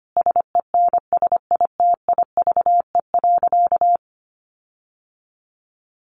Morse code, heard as a keyed tone, about 25 wpm, 700 Hz. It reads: SENSITIVE.
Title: Sentences from Top 1400 Words 25wpm